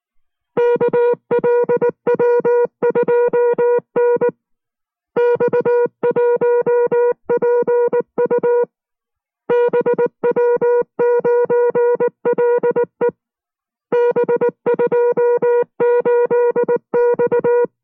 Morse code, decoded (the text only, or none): KLW2N X1PU BW9LE B28X